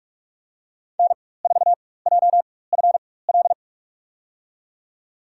Morse code, heard as a keyed tone, about 46 wpm, spaced as for 21 wpm, 700 Hz. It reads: N4JFL
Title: USA Call Signs - 5 Characters - 4x Character Spacing - 45wpm